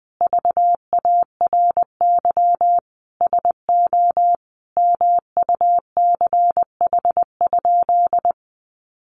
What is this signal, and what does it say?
Morse code, heard as a keyed tone, about 20 wpm, 700 Hz. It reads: VARY SO MUCH?
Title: Sentences from Top 1200 Words 20wpm